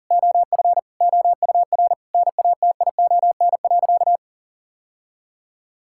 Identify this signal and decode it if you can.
Morse code, heard as a keyed tone, about 40 wpm, 700 Hz. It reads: OF OUR NATION.
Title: Sentences from Top 1000 Words - Review Part 6 - 40wpm